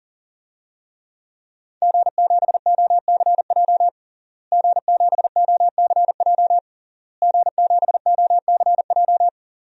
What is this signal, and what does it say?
Morse code, heard as a keyed tone, about 40 wpm, 700 Hz. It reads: G7OCJ G7OCJ G7OCJ
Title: DXCC-100 Call Signs - 5 Characters - Repeated 3x - 40wpm